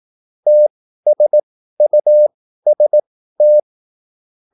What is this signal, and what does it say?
Morse code, heard as a keyed tone, about 18 wpm, spaced as for 13 wpm, 600 Hz. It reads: TSUST
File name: QSL / CW lesson